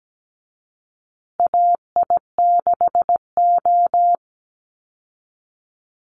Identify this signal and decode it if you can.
Morse code, heard as a keyed tone, about 17 wpm, 700 Hz. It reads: AI6O
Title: Call Signs Repeated 2x - 17wpm